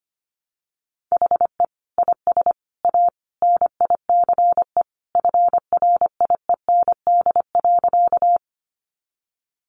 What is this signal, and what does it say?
Morse code, heard as a keyed tone, about 25 wpm, 700 Hz. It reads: HE IS A NICE FRIEND.
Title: Sentences from Top 1100 Words 25wpm